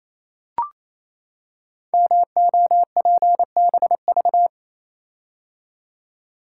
Morse code, heard as a keyed tone, about 27 wpm, 700 Hz. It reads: MOPBV
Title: Code Groups - 5 Letters - 28wpm